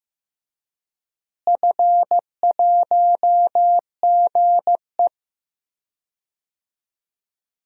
Morse code, heard as a keyed tone, about 15 wpm, 700 Hz. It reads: F1GE
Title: DXCC-100 Call Signs - 4 Characters - Repeated 3x - 15wpm